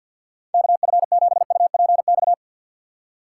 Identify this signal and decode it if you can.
Morse code, heard as a keyed tone, about 50 wpm, 700 Hz. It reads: KF7UPX
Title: USA Call Signs - 6 Characters - 8x Character Spacing - 50wpm